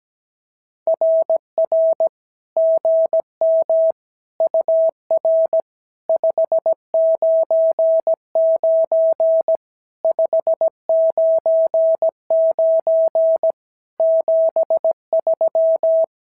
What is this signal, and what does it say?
Morse code, heard as a keyed tone, about 17 wpm, 650 Hz. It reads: RR GM UR 599 599 73
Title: SOTA QSO Practice 17wpm